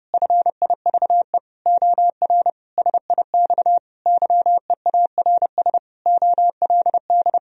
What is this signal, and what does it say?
Morse code, heard as a keyed tone, about 32 wpm, 700 Hz. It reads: FIVE OR SIX YEARS OLD
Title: Sentences from Top 1000 Words - Review Part 2 - 30wpm